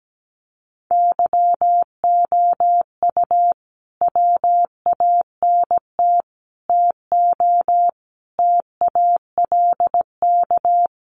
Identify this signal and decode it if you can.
Morse code, heard as a keyed tone, about 17 wpm, 700 Hz. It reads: YOU WANT TO TALK
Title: Sets of 5 Words Encore 17wpm